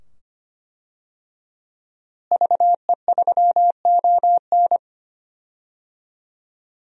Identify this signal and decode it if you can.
Morse code, heard as a keyed tone, about 25 wpm, 700 Hz. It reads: VE3ON
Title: DXCC-100 Call Signs - 5 Characters - Repeated 2x - 25wpm